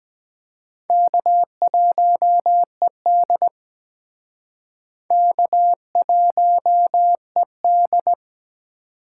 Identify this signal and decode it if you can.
Morse code, heard as a keyed tone, about 21 wpm, 700 Hz. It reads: K1ED K1ED
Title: Speed Racing - USA Call Signs 17wpm